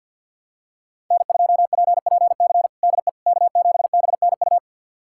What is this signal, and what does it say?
Morse code, heard as a keyed tone, about 49 wpm, 700 Hz. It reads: N2PPC DE K7BNU
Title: USA Callsign DE Callsign - 5 Characters - 50wpm